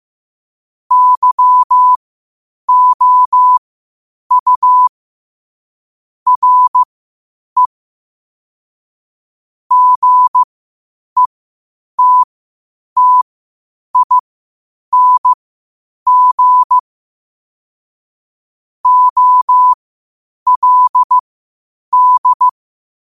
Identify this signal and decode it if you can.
Morse code, heard as a keyed tone, about 15 wpm, spaced as for 8 wpm, 1000 Hz. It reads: YOU RE GETTING OLD
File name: qotdfiles / 8wpm